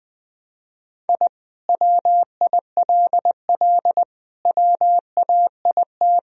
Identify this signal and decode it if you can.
Morse code, heard as a keyed tone, about 20 wpm, 700 Hz. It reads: I WILL WAIT
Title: Sentences from Top 1000 Words - Review Part 1 - 20wpm